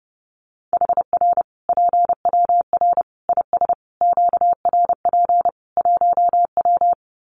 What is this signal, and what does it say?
Morse code, heard as a keyed tone, about 30 wpm, 700 Hz. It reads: HR PWR IS QRP 1W
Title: Common QSO Phrases 30wpm